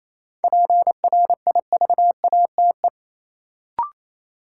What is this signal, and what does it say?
Morse code, heard as a keyed tone, about 28 wpm, 700 Hz. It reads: PRIVATE
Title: Sentences from Top 1000 Words - New Word Drill - 28wpm